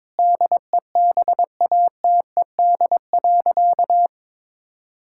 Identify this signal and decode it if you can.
Morse code, heard as a keyed tone, about 22 wpm, 700 Hz. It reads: DEBATED.
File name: Sentences with 7-Word Phrases - 22wpm